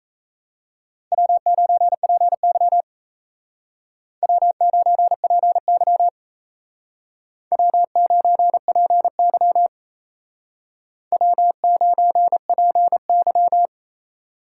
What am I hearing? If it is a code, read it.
Morse code, heard as a keyed tone, about 36 wpm, 700 Hz. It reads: W9PY W9PY W9PY W9PY